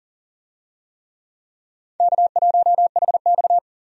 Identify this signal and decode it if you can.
Morse code, heard as a keyed tone, about 40 wpm, 700 Hz. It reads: K1HX
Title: USA Call Signs - 4 Characters - 40wpm